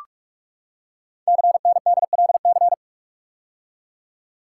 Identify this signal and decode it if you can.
Morse code, heard as a keyed tone, about 44 wpm, 700 Hz. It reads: CNDLC
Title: Code Groups - 5 Letters - 45wpm